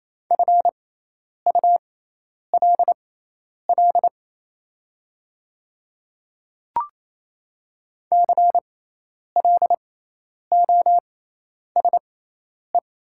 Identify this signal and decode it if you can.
Morse code, heard as a keyed tone, about 28 wpm, spaced as for 10 wpm, 700 Hz. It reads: FULL CLOSE
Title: 6x Character Spacing - Top 500 Words - 28wpm